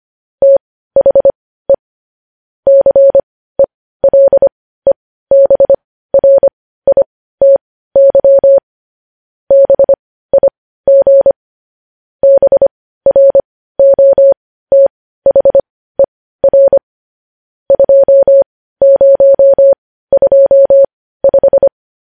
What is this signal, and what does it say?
Morse code, heard as a keyed tone, about 25 wpm, spaced as for 15 wpm, 550 Hz. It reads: THE CELEBRITY BIG BROTHER 2025
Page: News headlines at 15